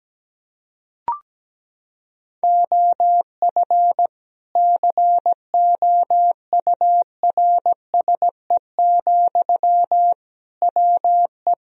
Morse code, heard as a keyed tone, about 17 wpm, 700 Hz. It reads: OF COURSE, WE
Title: Sentences from Top 1000 Words - Review Part 2 - 17wpm